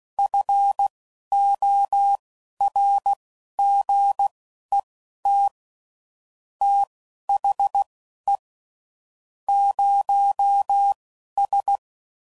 Morse code, heard as a keyed tone, about 16 wpm, spaced as for 11 wpm, 800 Hz. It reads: FORGET THE 0S
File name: CW Code Groups @10WPM / Technical Subjects / Gates Intro